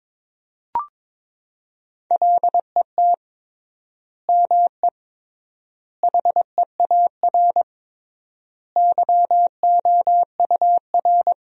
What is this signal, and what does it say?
Morse code, heard as a keyed tone, about 22 wpm, 700 Hz. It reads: LET ME HEAR YOUR
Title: 3x Word-Spacing - Sentences from Top 300 Words - 22wpm